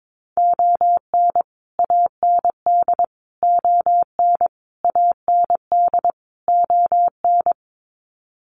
Morse code, heard as a keyed tone, about 22 wpm, 700 Hz. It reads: ON AND ON AND ON
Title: Sets of 5 Words Encore2 22wpm